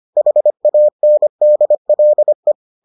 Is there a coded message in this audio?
Yes: Morse code from 0.2 to 2.5 s, heard as a keyed tone, about 25 wpm, 600 Hz. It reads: HANDLE